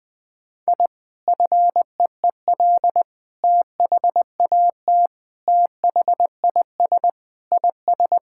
Morse code, heard as a keyed tone, about 20 wpm, 700 Hz. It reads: I FEEL THAT THIS IS
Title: Sentences with 4-Word Phrases - 20wpm